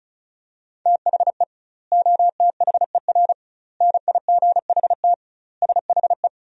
Morse code, heard as a keyed tone, about 36 wpm, 700 Hz. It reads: THE OTHER NIGHT SHE
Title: 2x Word-Spacing - Sentences from Top 300 Words - 35wpm